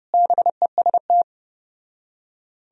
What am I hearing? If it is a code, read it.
Morse code, heard as a keyed tone, about 33 wpm, 700 Hz. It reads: BEST